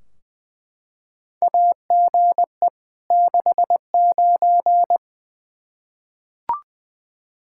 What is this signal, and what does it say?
Morse code, heard as a keyed tone, about 20 wpm, 700 Hz. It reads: AGE 69
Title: Practice QSO - Exchange 3 - More Info - 20wpm